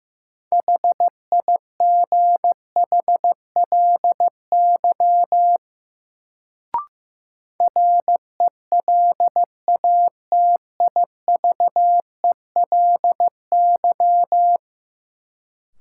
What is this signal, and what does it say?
Morse code, heard as a keyed tone, about 15 wpm, 700 Hz. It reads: HIGHLY RELATIVELY
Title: Sentences from Top 1100 Words - New Word Drill - 15wpm